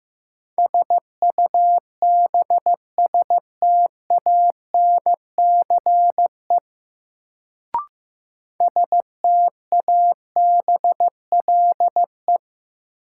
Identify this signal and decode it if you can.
Morse code, heard as a keyed tone, about 15 wpm, 700 Hz. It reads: SUBSTANCE STABLE
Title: Sentences from Top 1500 Words - New Word Drill Alt - 15wpm